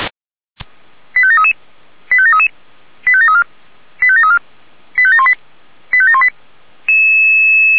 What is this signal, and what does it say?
Five-tone selective call (ZVEI), heard as quick runs of five tones, tones of about 70 ms. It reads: 85733 85733 85734 85734 85718 85718